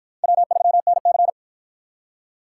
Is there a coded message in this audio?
Yes: Morse code from 0.2 to 1.3 s, heard as a keyed tone, about 53 wpm, 700 Hz. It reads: W3NC